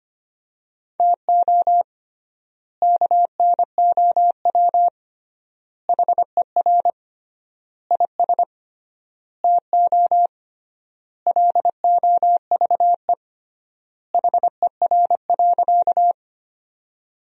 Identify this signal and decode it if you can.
Morse code, heard as a keyed tone, about 25 wpm, 700 Hz. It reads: TO KNOW HER IS TO LOVE HER.
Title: 3x Word-Spacing - Sentences from Top 400 Words - 25wpm